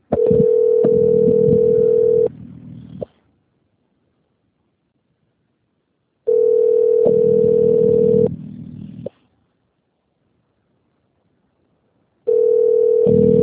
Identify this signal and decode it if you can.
Ringback tone (North America), 440 + 480 Hz.